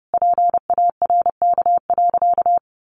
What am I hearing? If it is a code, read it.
Morse code, heard as a keyed tone, about 30 wpm, 700 Hz. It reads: PARK.
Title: Sentences from Top 1300 Words 30wpm